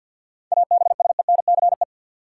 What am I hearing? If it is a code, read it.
Morse code, heard as a keyed tone, about 52 wpm, 700 Hz. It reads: ABSENCE
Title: Sets of 4 Words - New Word Drill - 50wpm